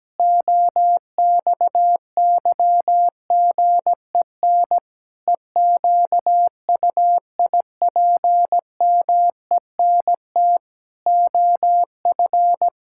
Morse code, heard as a keyed tone, about 17 wpm, 700 Hz. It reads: OXYGEN EQUIPMENT OF
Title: Pangram Challenge v2 - 17wpm